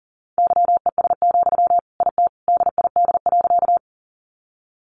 Morse code, heard as a keyed tone, about 40 wpm, 700 Hz. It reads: YES, IT DID.
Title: Sentences from Top 500 Words 40wpm